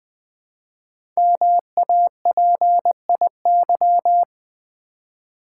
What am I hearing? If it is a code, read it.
Morse code, heard as a keyed tone, about 20 wpm, 700 Hz. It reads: MAPIY
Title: Code Groups - 5 Letters - 20wpm